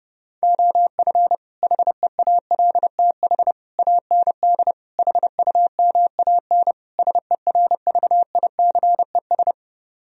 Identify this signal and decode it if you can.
Morse code, heard as a keyed tone, about 30 wpm, 700 Hz. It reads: OF HEALTH AND HUMAN SERVICES